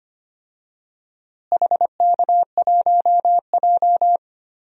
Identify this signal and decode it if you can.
Morse code, heard as a keyed tone, about 25 wpm, 700 Hz. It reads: HK1J
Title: DXCC-100 Call Signs - 4 Characters - Repeated 2x - 25wpm